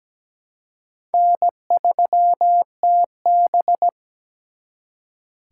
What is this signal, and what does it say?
Morse code, heard as a keyed tone, about 20 wpm, 700 Hz. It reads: N3TB
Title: Speed Racing - USA 1x2 Call Signs 17wpm